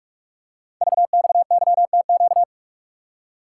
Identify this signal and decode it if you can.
Morse code, heard as a keyed tone, about 44 wpm, 690 Hz. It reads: VXYTQ
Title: Code Groups - 5 Letter:Numbers - 45wpm